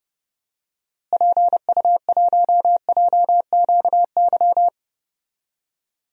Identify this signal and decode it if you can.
Morse code, heard as a keyed tone, about 31 wpm, 700 Hz. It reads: PU1JQY
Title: DXCC-100 Call Signs - 6 Characters - Repeated 3x - 30wpm